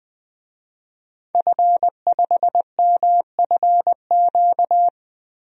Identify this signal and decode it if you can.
Morse code, heard as a keyed tone, about 21 wpm, 700 Hz. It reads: F5MFQ
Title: DXCC-100 Call Signs - 5 Characters - Repeated 2x - 20wpm